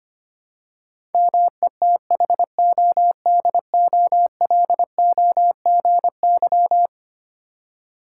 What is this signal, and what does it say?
Morse code, heard as a keyed tone, about 25 wpm, 700 Hz. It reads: METHODOLOGY
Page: morse-code-ninja-practice-25wpm